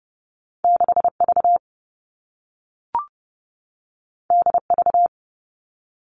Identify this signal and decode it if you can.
Morse code, heard as a keyed tone, about 30 wpm, 700 Hz. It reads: 6V DV
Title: All Cut Numbers - 2-Digit 30wpm